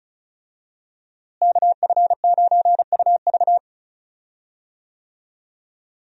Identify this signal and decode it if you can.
Morse code, heard as a keyed tone, about 36 wpm, 700 Hz. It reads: KF9UV
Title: USA Call Signs - 5 Characters - Repeated 3x - 35wpm